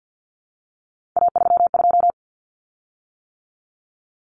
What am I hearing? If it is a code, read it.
Morse code, heard as a keyed tone, about 50 wpm, 700 Hz. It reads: A32
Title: Common Cut Numbers - 3-Digit 50wpm